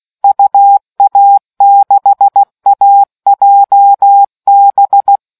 Morse code, heard as a keyed tone, about 16 wpm, 800 Hz. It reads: UA6AJB